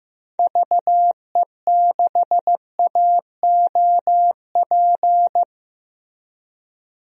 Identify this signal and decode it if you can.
Morse code, heard as a keyed tone, about 15 wpm, 700 Hz. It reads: VE6AOP
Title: DXCC-100 Call Signs - 6 Characters - Repeated 2x - 15wpm